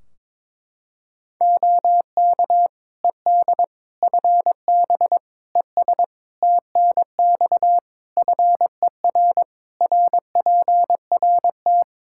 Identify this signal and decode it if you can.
Morse code, heard as a keyed tone, about 22 wpm, 700 Hz. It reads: OK ED FB ES TNX FER RPRT
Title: K1USN SST QSO Practice 22wpm